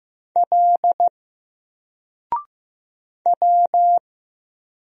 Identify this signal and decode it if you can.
Morse code, heard as a keyed tone, about 15 wpm, 700 Hz. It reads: L W
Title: Single Character (Letter:Number:Punctuation:Prosign) 15wpm